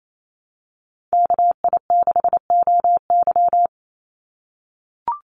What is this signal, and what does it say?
Morse code, heard as a keyed tone, about 28 wpm, 700 Hz. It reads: KI6OY
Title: Call Signs - 28wpm